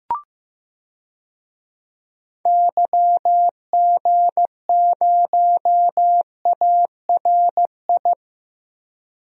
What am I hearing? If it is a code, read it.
Morse code, heard as a keyed tone, about 15 wpm, 700 Hz. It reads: YG0ARI
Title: DXCC-100 Call Signs - 6 Characters - Repeated 3x - 15wpm